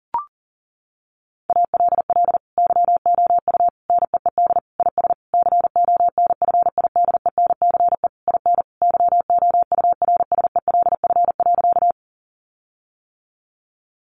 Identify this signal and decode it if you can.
Morse code, heard as a keyed tone, about 40 wpm, 700 Hz. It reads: ALL YOU NEED IS CONFIDENCE IN YOURSELF.
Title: Sentences from Top 1500 Words 40wpm